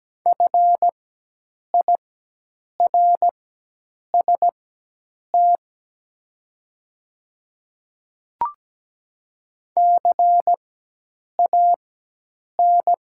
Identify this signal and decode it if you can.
Morse code, heard as a keyed tone, about 17 wpm, spaced as for 8 wpm, 700 Hz. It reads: FIRST CAN
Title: 4x Character Spacing - Top 100 Words - 17wpm